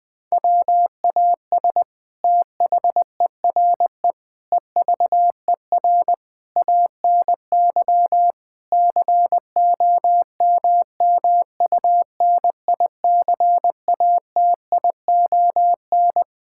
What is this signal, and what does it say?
Morse code, heard as a keyed tone, about 21 wpm, 700 Hz. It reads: WAS THERE EVER ANY COMMUNICATION
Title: Sentences from Top 1000 Words - Review Part 6 - 20wpm